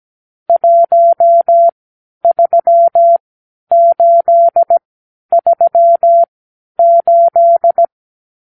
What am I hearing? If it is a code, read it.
Morse code, heard as a keyed tone, about 17 wpm, spaced as for 10 wpm, 680 Hz. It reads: 13838
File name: k7qo / Lesson 54 (9)